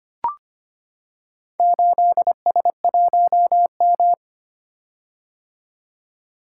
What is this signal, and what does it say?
Morse code, heard as a keyed tone, about 25 wpm, 700 Hz. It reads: 8S1M
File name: practice / 25wpm / Four Letter-Number Combinations - 25wpm